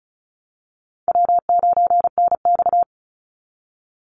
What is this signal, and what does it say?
Morse code, heard as a keyed tone, about 34 wpm, 700 Hz. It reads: W9NX
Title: Easy Call Signs Repeated 3x - 35wpm